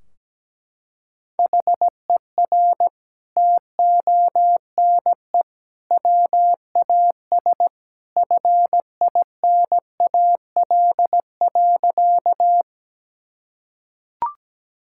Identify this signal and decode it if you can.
Morse code, heard as a keyed tone, about 17 wpm, 700 Hz. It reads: HER TONE WAS FINAL.
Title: Sentences from Top 1600 Words 17wpm